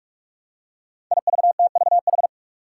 Morse code, heard as a keyed tone, about 46 wpm, 700 Hz. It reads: IVTVH
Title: Five Challenging Letter-Number Combinations 45wpm